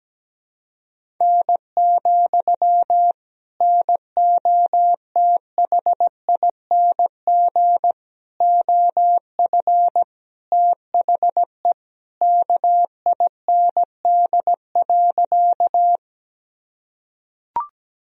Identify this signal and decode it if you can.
Morse code, heard as a keyed tone, about 17 wpm, 700 Hz. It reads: N, NOTHING OF THE KIND.
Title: Sentences from Top 500 Words and Abbreviations 17wpm